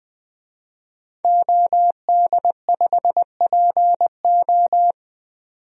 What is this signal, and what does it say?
Morse code, heard as a keyed tone, about 20 wpm, 700 Hz. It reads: OD5PO
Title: DXCC-100 Call Signs - 5 Characters - 20wpm